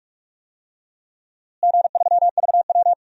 Morse code, heard as a keyed tone, about 46 wpm, 700 Hz. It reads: G3VW
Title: DXCC-100 Call Signs - 4 Characters - Repeated 2x - 45wpm